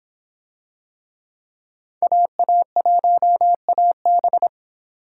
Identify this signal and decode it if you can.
Morse code, heard as a keyed tone, about 24 wpm, 700 Hz. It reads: AA1AB